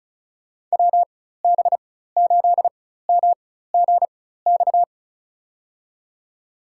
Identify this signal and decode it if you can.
Morse code, heard as a keyed tone, about 36 wpm, spaced as for 16 wpm, 700 Hz. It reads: WB8MGX
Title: USA Call Signs - 6 Characters - 4x Character Spacing - 35wpm